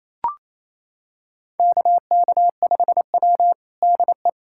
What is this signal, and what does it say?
Morse code, heard as a keyed tone, about 28 wpm, 700 Hz. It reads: KK5W DE
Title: USA Callsign DE Callsign - 4 Characters - 28wpm